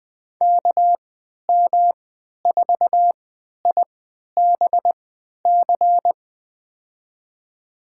Morse code, heard as a keyed tone, about 20 wpm, spaced as for 11 wpm, 700 Hz. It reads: KM4IBC
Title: USA Call Signs - 6 Characters - 3x Character Spacing - 20wpm